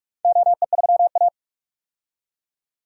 Morse code, heard as a keyed tone, about 43 wpm, 700 Hz. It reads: OE3A
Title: DXCC-100 Call Signs - 4 Characters - 45wpm